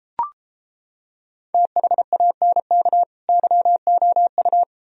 Morse code, heard as a keyed tone, about 29 wpm, 700 Hz. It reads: THANK YOU